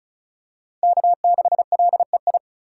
Morse code, heard as a keyed tone, about 36 wpm, 700 Hz. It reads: K6LEI
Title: USA Call Signs - 2x Character Spacing - 35wpm